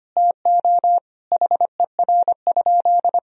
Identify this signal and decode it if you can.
Morse code, heard as a keyed tone, about 25 wpm, 700 Hz. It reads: TO HER?
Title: Sentences from Top 1000 Words - Review Part 3 - 25wpm